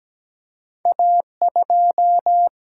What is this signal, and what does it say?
Morse code, heard as a keyed tone, about 17 wpm, 700 Hz. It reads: A2
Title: Single-Letter Single-Number 17wpm